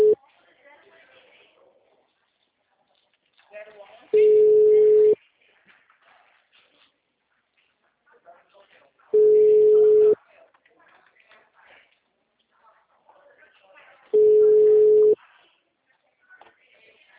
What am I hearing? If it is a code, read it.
Ringback tone (Europe, 425 Hz), with speech.
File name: out- -2016- - - .261